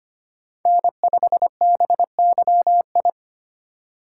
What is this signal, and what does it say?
Morse code, heard as a keyed tone, about 25 wpm, 700 Hz. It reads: N5BYI
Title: USA Call Signs - 5 Characters - 3x Character Spacing - 25wpm